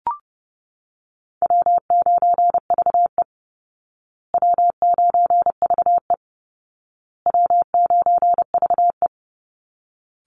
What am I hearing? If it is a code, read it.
Morse code, heard as a keyed tone, about 28 wpm, 700 Hz. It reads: W9VE W9VE W9VE
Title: Call Signs Repeated 2x - 30wpm